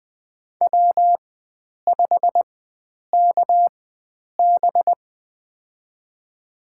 Morse code, heard as a keyed tone, about 20 wpm, spaced as for 9 wpm, 700 Hz. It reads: W5KB
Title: USA Call Signs - 4 Characters - 4x Character Spacing - 20wpm